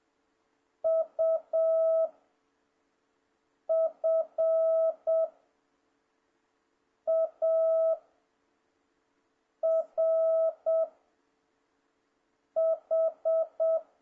Morse code, heard as a keyed tone, about 7 wpm, 650 Hz. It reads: U F A R H
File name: jh8cbh / 2026cw